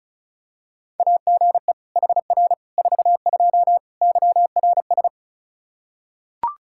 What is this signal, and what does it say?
Morse code, heard as a keyed tone, about 36 wpm, 700 Hz. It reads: AGE HR 42 YRS
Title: Practice QSO - Exchange 3 - More Info - 35wpm